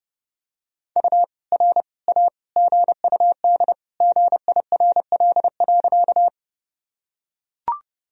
Morse code, heard as a keyed tone, about 32 wpm, 700 Hz. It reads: U R A GUD GIRL.
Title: Sentences from Top 400 Words and Abbreviations 30wpm